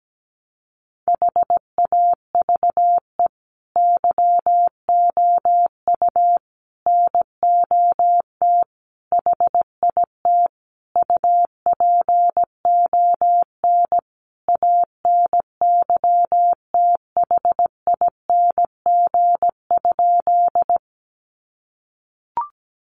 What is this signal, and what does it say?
Morse code, heard as a keyed tone, about 17 wpm, 700 Hz. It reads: HAVE YOU NOT HIT UPON ANYTHING?
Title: Sentences from Top 1200 Words 17wpm